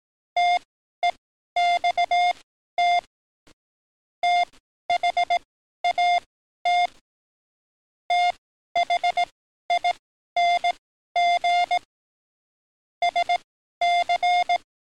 Morse code, heard as a keyed tone, about 18 wpm, spaced as for 12 wpm, 700 Hz. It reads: TEXT THAT THING SC
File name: training / NAVPERS 11049to11060 / NAVPERS 11053 RA RB